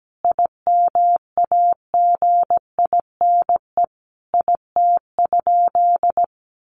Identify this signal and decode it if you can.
Morse code, heard as a keyed tone, about 17 wpm, 700 Hz. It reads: IMAGINE IT?
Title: Sentences from Top 1400 Words 17wpm